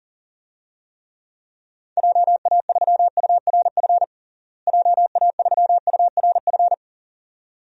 Morse code, heard as a keyed tone, about 40 wpm, 700 Hz. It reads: JA3URF JA3URF
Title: DXCC-100 Call Signs - 6 Characters - Repeated 2x - 40wpm